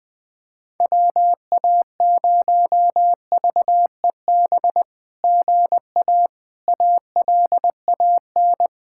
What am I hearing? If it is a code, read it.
Morse code, heard as a keyed tone, about 20 wpm, 700 Hz. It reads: WA0VEB GA ALAN
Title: SOTA QSO Practice 20wpm